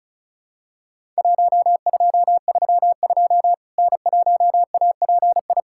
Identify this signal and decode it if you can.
Morse code, heard as a keyed tone, about 34 wpm, 700 Hz. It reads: 1232 N1API
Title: Worked All Europe DX Contest - QTC Exchange - 35wpm